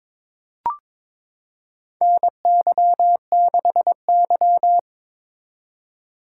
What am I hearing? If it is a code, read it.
Morse code, heard as a keyed tone, about 21 wpm, 700 Hz. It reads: NY6Y
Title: K1USN SST QSO Practice - 22wpm